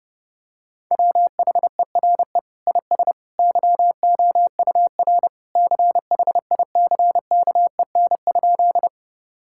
Morse code, heard as a keyed tone, about 30 wpm, 700 Hz. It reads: WHERE IS YOUR CHICKEN?
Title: Sentences from Top 1000 Words - Review Part 5 - 30wpm